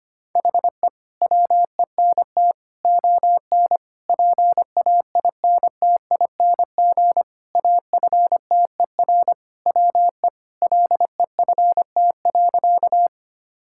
Morse code, heard as a keyed tone, about 25 wpm, 700 Hz. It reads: HE WENT ON PAINTING AFTER WE LEFT.
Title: Sentences from Top 1000 Words - Review Part 6 - 25wpm